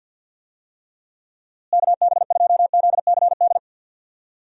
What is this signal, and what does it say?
Morse code, heard as a keyed tone, about 50 wpm, 700 Hz. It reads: KB2ZCD
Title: USA Call Signs - 6 Characters - Repeated 2x - 50wpm